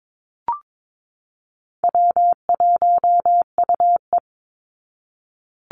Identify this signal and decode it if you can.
Morse code, heard as a keyed tone, about 22 wpm, 700 Hz. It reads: W1UE
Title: Call Signs Repeated 3x - 22wpm